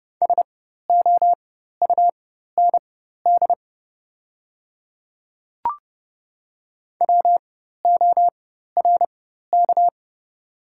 Morse code, heard as a keyed tone, about 31 wpm, spaced as for 14 wpm, 700 Hz. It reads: SOUND WORK
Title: 4x Character Spacing - Top 300 Words - 30wpm